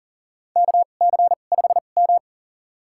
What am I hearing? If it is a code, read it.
Morse code, heard as a keyed tone, about 40 wpm, spaced as for 29 wpm, 700 Hz. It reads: KC5M
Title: USA Call Signs - 4 Characters - 2x Character Spacing - 40wpm